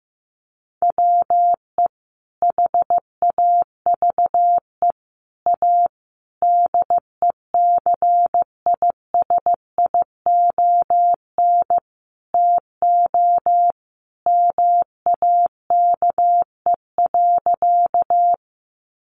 Morse code, heard as a keyed tone, about 14 wpm, 700 Hz. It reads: WE HAVE A DECISION TO MAKE.
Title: Sentences from Top 1000 Words 15wpm